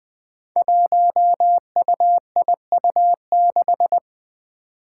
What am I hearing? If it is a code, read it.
Morse code, heard as a keyed tone, about 20 wpm, 700 Hz. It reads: 1UIU6
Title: Five Challenging Letter-Number Combinations 20wpm